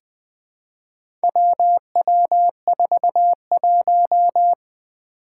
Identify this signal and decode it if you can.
Morse code, heard as a keyed tone, about 20 wpm, 700 Hz. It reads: WW41
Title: Four Challenging Letter-Number Combinations 20wpm